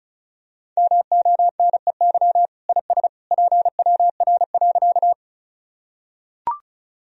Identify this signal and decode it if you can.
Morse code, heard as a keyed tone, about 34 wpm, 700 Hz. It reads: MONEY IS PWR.
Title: Sentences from Top 500 Words and Abbreviations 35wpm